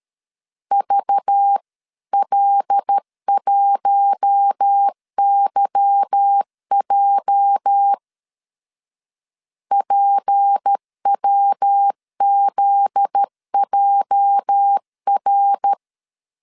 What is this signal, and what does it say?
Morse code, heard as a keyed tone, about 13 wpm, 800 Hz. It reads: VL1YJ PWZJR
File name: LEZIONI / 8 YJ45 / 8 YJ 12wpm 2235